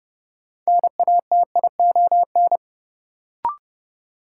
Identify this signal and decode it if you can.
Morse code, heard as a keyed tone, about 30 wpm, 700 Hz. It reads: NATION
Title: Sentences from Top 900 Words - New Word Drill - 30wpm